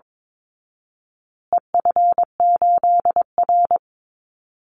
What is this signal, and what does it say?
Morse code, heard as a keyed tone, about 22 wpm, 700 Hz. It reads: EF8R
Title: Call Signs Repeated 2x - 22wpm